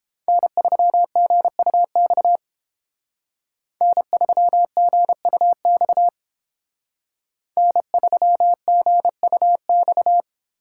Morse code, heard as a keyed tone, about 31 wpm, 700 Hz. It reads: N3GUX N3GUX N3GUX